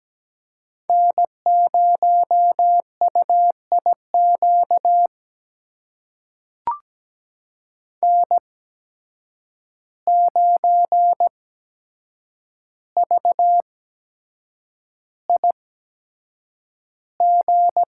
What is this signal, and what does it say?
Morse code, heard as a keyed tone, about 17 wpm, spaced as for 7 wpm, 700 Hz. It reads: N0UIQ N9VIG
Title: USA Call Signs - 5 Characters - 8x Character Spacing - 17wpm